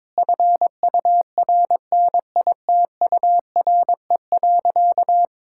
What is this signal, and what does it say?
Morse code, heard as a keyed tone, about 22 wpm, 700 Hz. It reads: FURNITURE.
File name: practice / 22wpm / Sentences with 4-Word Phrases - 22wpm